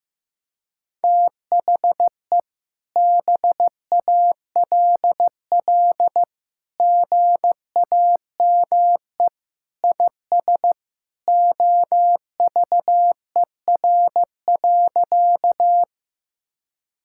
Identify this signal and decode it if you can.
Morse code, heard as a keyed tone, about 16 wpm, 700 Hz. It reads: THE BALL GAME IS OVER.
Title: Sentences from Top 1000 Words - Review Part 6 - 15wpm